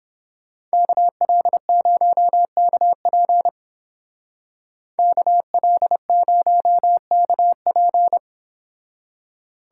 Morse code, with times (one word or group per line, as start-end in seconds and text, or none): 0.7-3.5 KL0KP
5.0-8.2 KL0KP